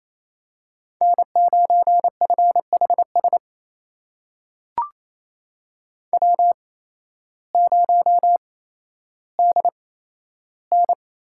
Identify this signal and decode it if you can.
Morse code, heard as a keyed tone, about 28 wpm, spaced as for 13 wpm, 700 Hz. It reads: N9FHS W0DN
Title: USA Call Signs - 5 Characters - 8x Character Spacing - 28wpm